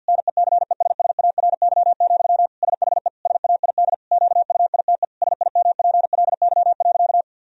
Morse code, heard as a keyed tone, about 52 wpm, 700 Hz. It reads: NECESSARY, SHE SAID QUITE SIMPLY.
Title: Sentences from Top 1000 Words - Review Part 4 - 50wpm